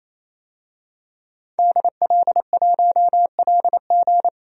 Morse code, heard as a keyed tone, about 28 wpm, 700 Hz. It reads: DL1LG
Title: DXCC-100 Call Signs - 5 Characters - 28wpm